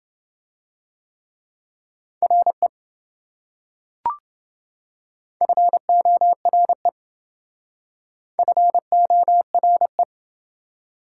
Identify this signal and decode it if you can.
Morse code, heard as a keyed tone, about 28 wpm, 700 Hz. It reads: RE FORE FORE